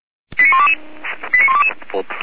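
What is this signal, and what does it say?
Five-tone selective call (ZVEI), heard as quick runs of five tones, tones of about 70 ms, with speech. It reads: 89122 89122
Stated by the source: Five-tone ZVEI selective call received on an SDR.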